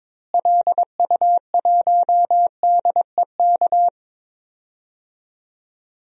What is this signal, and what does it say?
Morse code, heard as a keyed tone, about 22 wpm, 700 Hz. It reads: LU1DEK
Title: DXCC-100 Call Signs - 6 Characters - Repeated 3x - 22wpm